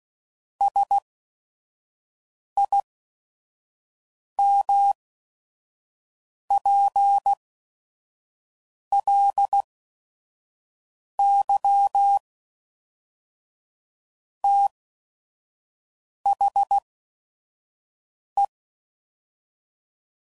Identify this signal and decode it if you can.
Morse code, heard as a keyed tone, about 16 wpm, spaced as for 5 wpm, 800 Hz. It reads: SIMPLY THE
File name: CW Code Groups @5WPM / Technical Subjects / XL